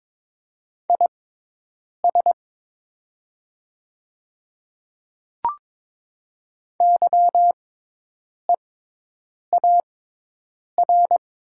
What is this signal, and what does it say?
Morse code, heard as a keyed tone, about 22 wpm, spaced as for 8 wpm, 700 Hz. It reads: IS YEAR
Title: 6x Character Spacing - Top 300 Words - 22wpm